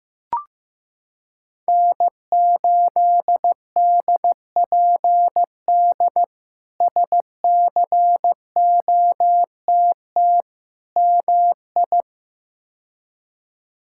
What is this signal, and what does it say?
Morse code, heard as a keyed tone, about 15 wpm, 700 Hz. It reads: N8DPD SCOTT MI
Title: K1USN SST QSO Practice - 15wpm